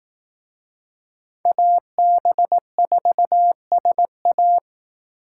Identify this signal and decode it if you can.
Morse code, heard as a keyed tone, about 18 wpm, 700 Hz. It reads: AB4SA
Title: Speed Racing - USA Call Signs 15wpm